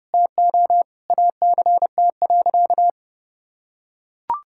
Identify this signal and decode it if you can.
Morse code, heard as a keyed tone, about 32 wpm, 700 Hz. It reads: TO ACT.
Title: Sentences from Top 1000 Words - Review Part 6 - 30wpm